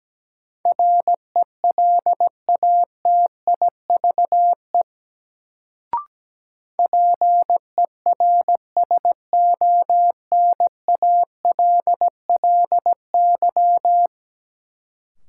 Morse code, heard as a keyed tone, about 17 wpm, 700 Hz. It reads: RELATIVE PERSONALLY
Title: Sentences from Top 1400 Words - New Word Drill Alt - 17wpm